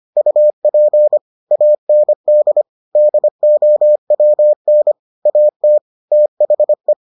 Morse code, heard as a keyed tone, about 25 wpm, 600 Hz. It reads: UP AND DOWN AT THE